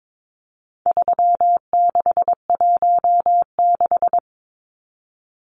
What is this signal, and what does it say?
Morse code, heard as a keyed tone, about 22 wpm, 700 Hz. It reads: 3616
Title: Numbers - 4-Digit 22wpm